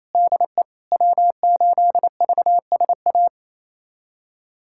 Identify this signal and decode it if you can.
Morse code, heard as a keyed tone, about 28 wpm, 690 Hz. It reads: DE W8VSA
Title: USA Callsign DE Callsign - 5 Characters - 28wpm